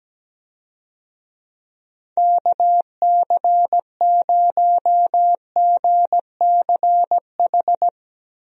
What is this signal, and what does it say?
Morse code, heard as a keyed tone, about 17 wpm, 700 Hz. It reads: KC0GCH
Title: USA Call Signs - 6 Characters - Repeated 3x - 17wpm